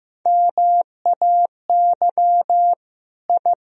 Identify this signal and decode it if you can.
Morse code, heard as a keyed tone, about 15 wpm, 700 Hz. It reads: MAY I
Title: Sentences from Top 1000 Words - Review Part 1 - 15wpm